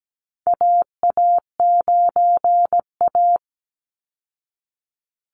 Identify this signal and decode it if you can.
Morse code, heard as a keyed tone, about 17 wpm, 700 Hz. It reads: AA9A
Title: Call Signs Repeated 2x - 17wpm